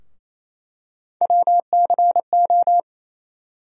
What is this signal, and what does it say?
Morse code, heard as a keyed tone, about 28 wpm, 700 Hz. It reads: WCO
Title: USA Call Sign Postfixes - 3 Letters 28wpm